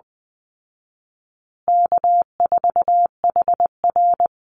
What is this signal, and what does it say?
Morse code, heard as a keyed tone, about 20 wpm, 700 Hz. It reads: K4HR
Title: Call Signs Repeated 3x - 20wpm